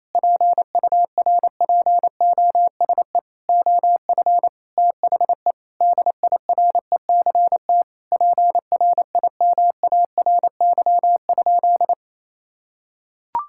Morse code, heard as a keyed tone, about 28 wpm, 700 Hz. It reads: PURPOSE OF THE DIRECT PRIMARY?
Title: Sentences from Top 1000 Words - Review Part 5 - 28wpm